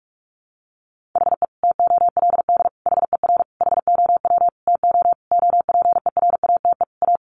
Morse code, heard as a keyed tone, about 44 wpm, 700 Hz. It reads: HE TOLD HER HOW TO OPERATE A
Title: Sentences from Top 1500 Words 45wpm